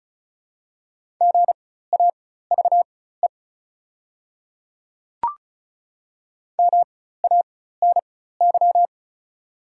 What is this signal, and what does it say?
Morse code, heard as a keyed tone, about 36 wpm, spaced as for 16 wpm, 690 Hz. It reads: GAVE MANY
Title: 4x Character Spacing - Top 500 Words - 35wpm